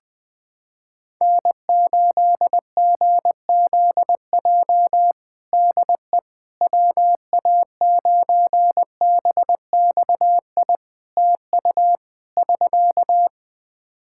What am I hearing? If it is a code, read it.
Morse code, heard as a keyed tone, about 20 wpm, 700 Hz. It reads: N8GZJ DE WA9BXI TU <SK>
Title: K1USN SST QSO Practice 20wpm